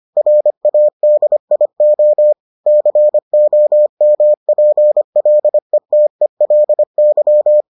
Morse code, heard as a keyed tone, about 25 wpm, 600 Hz. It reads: RADIO COMPLETELY